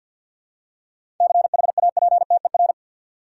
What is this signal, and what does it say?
Morse code, heard as a keyed tone, about 52 wpm, 700 Hz. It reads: CHAPTER
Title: Seven Letter Words - 50wpm